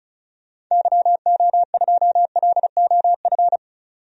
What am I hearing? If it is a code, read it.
Morse code, heard as a keyed tone, about 34 wpm, 700 Hz. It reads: YO2LOF
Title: DXCC-100 Call Signs - 6 Characters - 35wpm